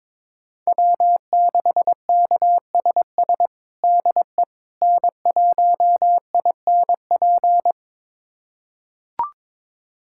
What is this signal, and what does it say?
Morse code, heard as a keyed tone, about 22 wpm, 700 Hz. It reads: W6KSS DE N1INP
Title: USA Callsign DE Callsign - 5 Characters - 22wpm